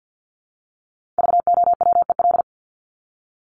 Morse code, heard as a keyed tone, about 50 wpm, 700 Hz. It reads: VOWEL
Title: Top 500 Words Repeated 50wpm